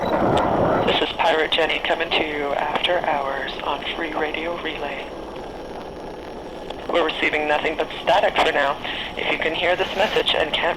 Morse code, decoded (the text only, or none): none